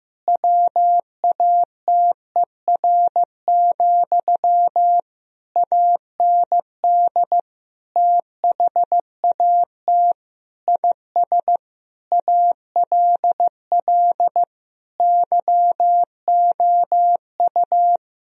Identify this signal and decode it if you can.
Morse code, heard as a keyed tone, about 15 wpm, 700 Hz. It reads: WATER, AND THAT IS ALL YOU
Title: Sentences from Top 1000 Words - Review Part 2 - 15wpm